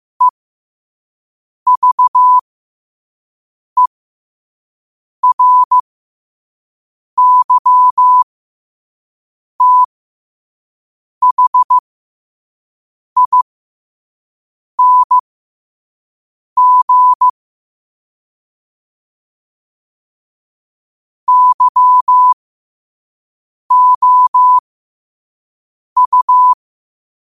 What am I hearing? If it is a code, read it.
Morse code, heard as a keyed tone, about 15 wpm, spaced as for 5 wpm, 1000 Hz. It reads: EVERYTHING YOU